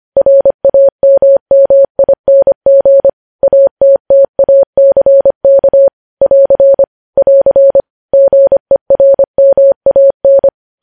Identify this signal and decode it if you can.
Morse code, heard as a keyed tone, about 25 wpm, 550 Hz. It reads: RAMMING ATTACK <AR> <AR> GERMAN
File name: podcast / News headlines-1x-25x25